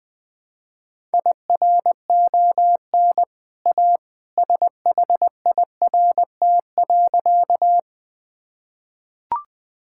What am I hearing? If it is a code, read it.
Morse code, heard as a keyed tone, about 20 wpm, 700 Hz. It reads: IRON A SHIRT.
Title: Easy Sentences with Hints - 20wpm